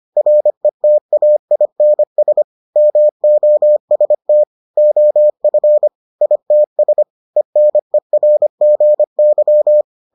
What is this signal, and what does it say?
Morse code, heard as a keyed tone, about 25 wpm, 600 Hz. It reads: RETAINS MOST OF ITS ENERGY